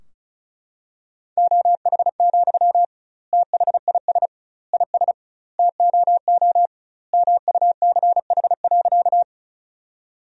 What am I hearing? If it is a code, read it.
Morse code, heard as a keyed tone, about 34 wpm, 700 Hz. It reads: OH, THIS IS TOO MUCH.
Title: 2x Word-Spacing - Sentences from Top 500 Words - 35wpm